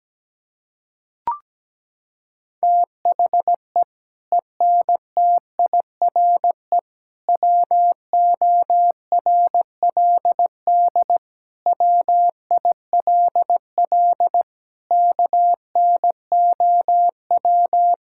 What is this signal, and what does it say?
Morse code, heard as a keyed tone, about 17 wpm, 700 Hz. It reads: THE ENTIRE WORLD WILL KNOW
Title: Sentences from Top 1000 Words - Review Part 5 - 17wpm